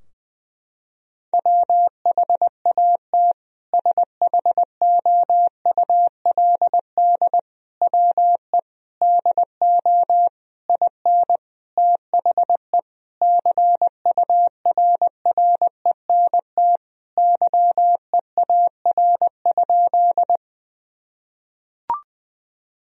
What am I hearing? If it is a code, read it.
Morse code, heard as a keyed tone, about 21 wpm, 700 Hz. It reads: WHAT SHOULD WE DO IN THE CURRENT YEAR?
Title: Sentences from Top 1000 Words - Review Part 4 - 20wpm